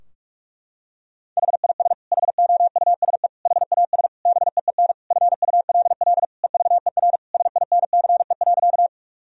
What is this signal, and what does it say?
Morse code, heard as a keyed tone, about 43 wpm, 700 Hz. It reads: HIS HOUSE HAS BEEN FULL EVER SINCE.
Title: Sentences from Top 1000 Words - Review Part 3 - 45wpm